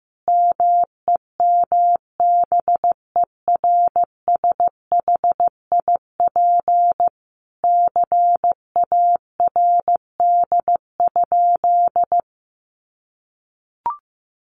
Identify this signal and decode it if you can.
Morse code, heard as a keyed tone, about 15 wpm, 700 Hz. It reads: MEMBERSHIP CARD?
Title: Sentences from Top 1400 Words 15wpm